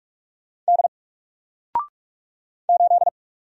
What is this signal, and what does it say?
Morse code, heard as a keyed tone, about 46 wpm, 700 Hz. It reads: D 8
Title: Single Letter-Number Alt 45wpm